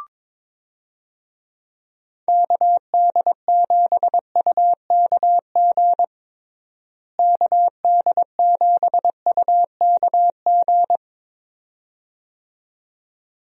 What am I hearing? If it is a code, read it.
Morse code, heard as a keyed tone, about 22 wpm, 700 Hz. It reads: KD7UKG KD7UKG
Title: USA Call Signs - 6 Characters - Repeated 2x - 22wpm